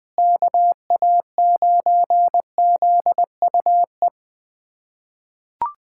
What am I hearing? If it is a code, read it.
Morse code, heard as a keyed tone, about 20 wpm, 700 Hz. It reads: KA9ZUE
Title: USA Callsign DE Callsign 20wpm